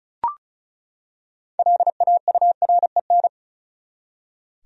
Morse code, heard as a keyed tone, about 36 wpm, 700 Hz. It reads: LAUREN
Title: USA Common Names - 35wpm